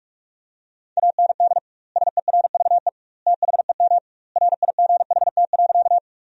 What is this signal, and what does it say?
Morse code, heard as a keyed tone, about 46 wpm, 700 Hz. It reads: AND SERVE THEM RIGHT.
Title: 2x Word-Spacing - Sentences from Top 400 Words - 45wpm